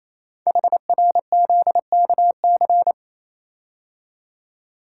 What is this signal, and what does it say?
Morse code, heard as a keyed tone, about 28 wpm, 700 Hz. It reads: HRZKC
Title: Code Groups - 5 Letters - 28wpm